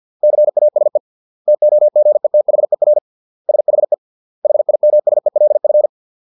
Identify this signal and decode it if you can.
Morse code, heard as a keyed tone, about 50 wpm, 600 Hz. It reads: CASE TOGETHER SHE HIMSELF